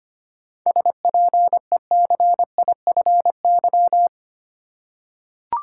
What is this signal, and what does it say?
Morse code, heard as a keyed tone, about 25 wpm, 700 Hz. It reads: SPECIFY
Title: Seven Letter Words - 25wpm